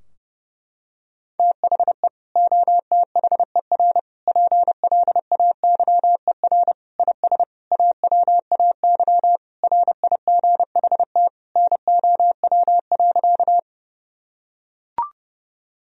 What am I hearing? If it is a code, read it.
Morse code, heard as a keyed tone, about 28 wpm, 700 Hz. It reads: THE OTHER PLAYER IS AWAY RIGHT NOW.
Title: Sentences from Top 1000 Words - Review Part 4 - 30wpm